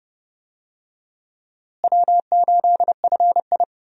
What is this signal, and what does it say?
Morse code, heard as a keyed tone, about 30 wpm, 700 Hz. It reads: W8FI
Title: USA Call Signs - 4 Characters - Repeated 3x - 30wpm